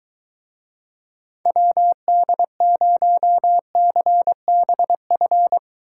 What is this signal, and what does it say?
Morse code, heard as a keyed tone, about 23 wpm, 700 Hz. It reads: WD0CBF